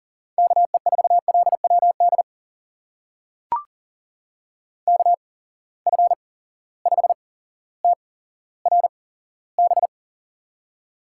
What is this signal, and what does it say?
Morse code, heard as a keyed tone, about 38 wpm, spaced as for 17 wpm, 700 Hz. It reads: KE4LWD KF5TRB
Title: USA Call Signs - 6 Characters - 8x Character Spacing - 40wpm